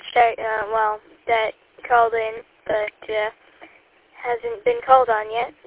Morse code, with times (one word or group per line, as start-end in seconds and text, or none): none